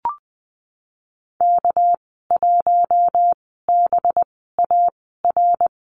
Morse code, heard as a keyed tone, about 20 wpm, spaced as for 15 wpm, 700 Hz. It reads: K1BAR
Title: 200 Lesson 32g - Letter k - Call Signs 20wpm